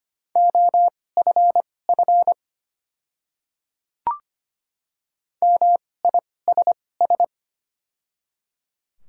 Morse code, heard as a keyed tone, about 25 wpm, spaced as for 18 wpm, 700 Hz. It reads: OFF MISS
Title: 2x Character Spacing - Top 500 Words - 25wpm